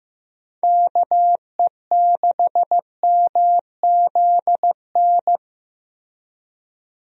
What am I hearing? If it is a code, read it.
Morse code, heard as a keyed tone, about 16 wpm, 700 Hz. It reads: KE6MZN
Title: USA Call Signs - 6 Characters - Repeated 3x - 15wpm